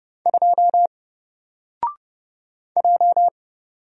Morse code, heard as a keyed tone, about 30 wpm, 700 Hz. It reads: 2 J
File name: Single-Letter-Number-30wpm